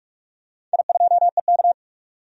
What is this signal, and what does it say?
Morse code, heard as a keyed tone, about 46 wpm, 700 Hz. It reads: I2EK
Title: DXCC-100 Call Signs - 4 Characters - 45wpm